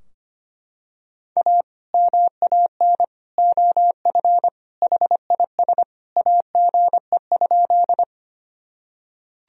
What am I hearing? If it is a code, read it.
Morse code, heard as a keyed tone, about 25 wpm, 700 Hz. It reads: A MAN OF HIS AGE?
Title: Sentences from Top 1000 Words - Review Part 3 - 25wpm